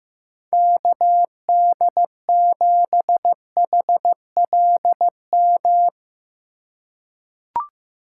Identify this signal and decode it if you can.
Morse code, heard as a keyed tone, about 15 wpm, 700 Hz. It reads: KD7HLM
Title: USA Call Signs - 6 Characters - 3x Character Spacing - 15wpm